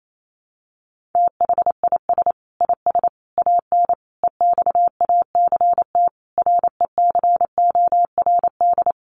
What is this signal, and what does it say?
Morse code, heard as a keyed tone, about 28 wpm, 700 Hz. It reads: THIS IS AN EXACT RECORD
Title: Sentences from Top 1300 Words 28wpm